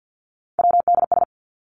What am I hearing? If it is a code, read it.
Morse code, heard as a keyed tone, about 50 wpm, 700 Hz. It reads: WDS
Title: Abbreviations Repeated 3x - 50wpm